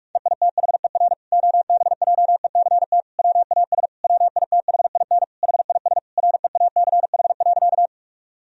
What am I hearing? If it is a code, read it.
Morse code, heard as a keyed tone, about 41 wpm, 700 Hz. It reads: EITHER OBJECT WAS WITHIN HIS REACH.
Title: Sentences from Top 1000 Words - Review Part 4 - 45wpm